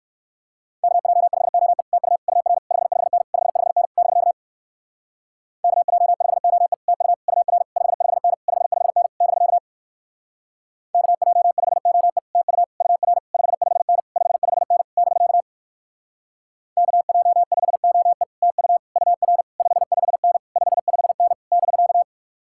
Morse code, heard as a keyed tone, about 61 wpm, 700 Hz. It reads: KJ5OE TU UR 55N 55N <BK> KJ5OE TU UR 55N 55N <BK> KJ5OE TU UR 55N 55N <BK> KJ5OE TU UR 55N 55N <BK>